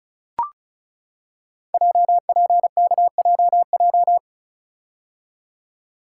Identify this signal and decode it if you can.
Morse code, heard as a keyed tone, about 36 wpm, 700 Hz. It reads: JPKJJ
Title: Code Groups - 5 Letter:Numbers - 35wpm